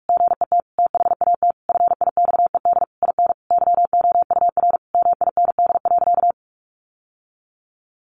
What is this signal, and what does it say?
Morse code, heard as a keyed tone, about 46 wpm, 700 Hz. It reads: GET THAT FIXED IN YOUR MIND.
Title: Sentences from Top 1100 Words 45wpm